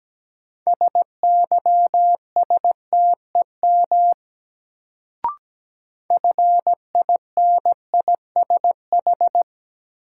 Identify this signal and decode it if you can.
Morse code, heard as a keyed tone, about 17 wpm, 700 Hz. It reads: SYSTEM FINISH
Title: Sentences from Top 1000 Words - Review Part 3 - New Word Drill - 17wpm